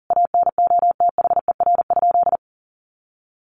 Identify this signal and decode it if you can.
Morse code, heard as a keyed tone, about 40 wpm, 700 Hz. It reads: ANOTHER?
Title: Sentences from Top 1400 Words 40wpm